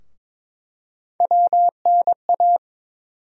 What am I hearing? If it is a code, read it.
Morse code, heard as a keyed tone, about 22 wpm, 700 Hz. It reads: WNA
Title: USA Call Sign Postfixes - 3 Letters 22wpm